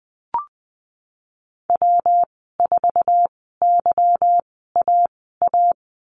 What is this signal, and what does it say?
Morse code, heard as a keyed tone, about 20 wpm, spaced as for 14 wpm, 700 Hz. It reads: W4YAA
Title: 147 Lesson 24g - Letter y - Call Signs 20wpm